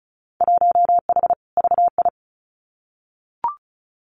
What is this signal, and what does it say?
Morse code, heard as a keyed tone, about 34 wpm, 700 Hz. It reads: 1H VI
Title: Winter Field Day Exchanges Encore 35wpm